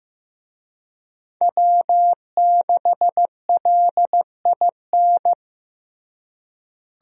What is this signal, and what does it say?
Morse code, heard as a keyed tone, about 15 wpm, 700 Hz. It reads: W6LIN
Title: USA Call Signs - 5 Characters - Repeated 2x - 15wpm